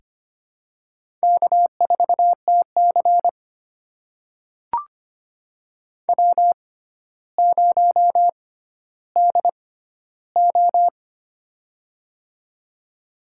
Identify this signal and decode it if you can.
Morse code, heard as a keyed tone, about 25 wpm, spaced as for 13 wpm, 700 Hz. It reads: K4TC W0DO
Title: USA Call Signs - 4 Characters - 6x Character Spacing - 25wpm